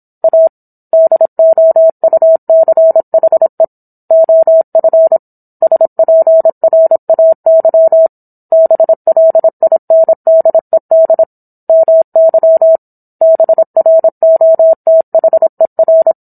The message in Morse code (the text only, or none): A DOUCHE OF SPRAY BLINDED MY BROTHER